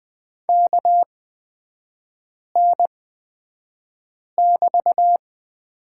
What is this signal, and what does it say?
Morse code, heard as a keyed tone, about 20 wpm, spaced as for 5 wpm, 700 Hz. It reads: KN<BT>
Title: 224 Lesson 36e - BT - Three Characters 20wpm